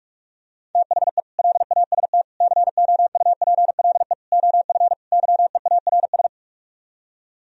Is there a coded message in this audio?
Yes: Morse code from 0.8 to 6.3 s, heard as a keyed tone, about 49 wpm, 700 Hz. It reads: THE LAST COUPLE OF YEARS